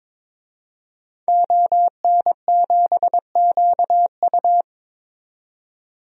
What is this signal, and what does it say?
Morse code, heard as a keyed tone, about 22 wpm, 700 Hz. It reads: ON7QU
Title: DXCC-100 Call Signs - 5 Characters - 22wpm